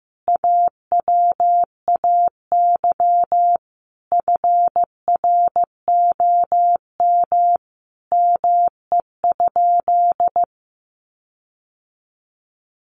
Morse code, heard as a keyed tone, about 15 wpm, 700 Hz. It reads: AWAY FROM ME?
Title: Sentences from Top 1500 Words 15wpm